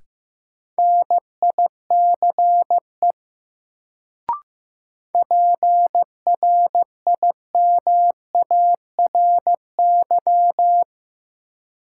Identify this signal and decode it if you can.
Morse code, heard as a keyed tone, about 15 wpm, 700 Hz. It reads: NICE PRIMARY
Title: Sentences from Top 1100 Words - New Word Drill Alt - 15wpm